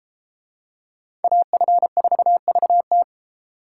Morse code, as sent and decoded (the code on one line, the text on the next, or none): .- ..-. ....- ...- -
AF4VT